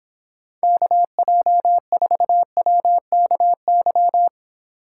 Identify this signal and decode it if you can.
Morse code, heard as a keyed tone, about 26 wpm, 700 Hz. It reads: KJ4WKY